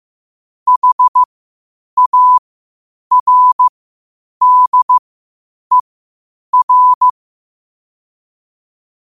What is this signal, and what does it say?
Morse code, heard as a keyed tone, about 15 wpm, spaced as for 8 wpm, 1000 Hz. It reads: HARDER
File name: qotdfiles / 8wpm